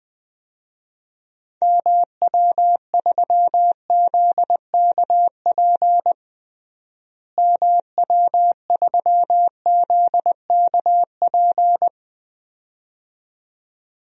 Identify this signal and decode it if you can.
Morse code, heard as a keyed tone, about 21 wpm, 700 Hz. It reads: MW3ZKP MW3ZKP
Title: DXCC-100 Call Signs - 6 Characters - Repeated 2x - 20wpm